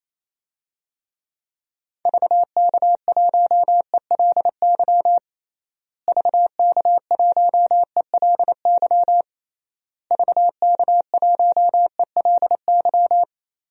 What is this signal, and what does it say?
Morse code, heard as a keyed tone, about 27 wpm, 700 Hz. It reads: VK1ELY VK1ELY VK1ELY
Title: DXCC-100 Call Signs - 6 Characters - Repeated 3x - 28wpm